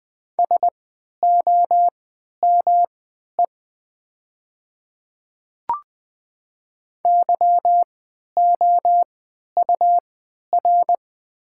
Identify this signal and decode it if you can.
Morse code, heard as a keyed tone, about 20 wpm, spaced as for 11 wpm, 700 Hz. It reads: SOME YOUR
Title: 3x Character Spacing - Top 100 Words - 20wpm